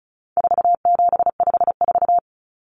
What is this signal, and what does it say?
Morse code, heard as a keyed tone, about 34 wpm, 700 Hz. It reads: 4754
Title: Common Cut Numbers - 4-Digit 35wpm